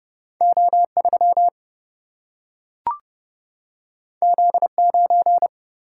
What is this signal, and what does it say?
Morse code, heard as a keyed tone, about 30 wpm, 700 Hz. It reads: O3 Z9
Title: Single-Letter Single-Number 30wpm